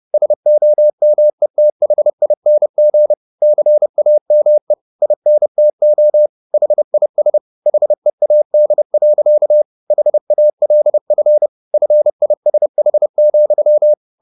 Morse code, heard as a keyed tone, about 30 wpm, 600 Hz. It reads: SOMETHING CAME INTO HIS HEAD. HALF FISH,